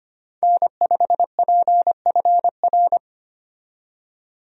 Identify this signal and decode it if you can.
Morse code, heard as a keyed tone, about 25 wpm, 700 Hz. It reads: N5PFR
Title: USA Call Signs - 5 Characters - 25wpm